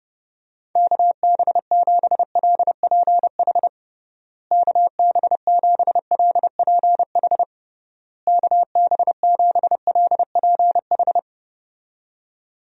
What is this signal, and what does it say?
Morse code, heard as a keyed tone, about 30 wpm, 700 Hz. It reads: KB7LPH KB7LPH KB7LPH
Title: USA Call Signs - 6 Characters - Repeated 3x - 30wpm